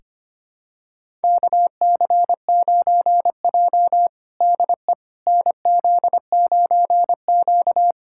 Morse code, heard as a keyed tone, about 25 wpm, 700 Hz. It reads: KC9J DE NZ9Q
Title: USA Callsign DE Callsign - 4 Characters - 25wpm